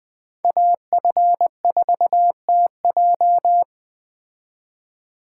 Morse code, heard as a keyed tone, about 20 wpm, 700 Hz. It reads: AF4TJ